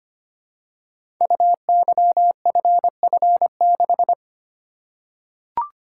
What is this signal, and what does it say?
Morse code, heard as a keyed tone, about 25 wpm, 700 Hz. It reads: UYFF6
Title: Code Groups - 5 Letter:Numbers - 25wpm